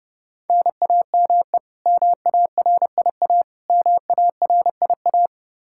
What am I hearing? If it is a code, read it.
Morse code, heard as a keyed tone, about 30 wpm, 700 Hz. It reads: NAME MARIA MARIA
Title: Practice QSO - Exchange 1 - Beginning - 30wpm